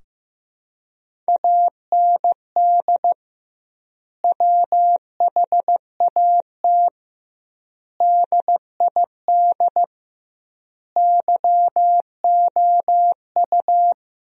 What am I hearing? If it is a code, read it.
Morse code, heard as a keyed tone, about 16 wpm, 700 Hz. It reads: AND WHAT DID YOU
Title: 2x Word-Spacing - Sentences from Top 100 Words - 15wpm